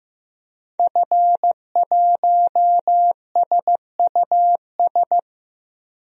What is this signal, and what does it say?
Morse code, heard as a keyed tone, about 15 wpm, 700 Hz. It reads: F1SUS
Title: DXCC-100 Call Signs - 5 Characters - Repeated 3x - 15wpm